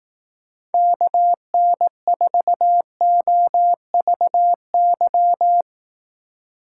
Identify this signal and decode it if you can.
Morse code, heard as a keyed tone, about 18 wpm, 700 Hz. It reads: KN4OVY
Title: Speed Racing - USA 2x3 Call Signs 15wpm